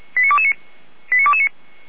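Five-tone selective call (ZVEI), heard as quick runs of five tones, tones of about 70 ms. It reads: 89229
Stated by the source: ZVEI five-tone call received on an SDR by feuerwehr-orlamuende.org (fire brigade Orlamünde).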